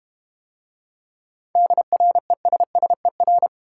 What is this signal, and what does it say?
Morse code, heard as a keyed tone, about 32 wpm, 700 Hz. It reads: DRESSER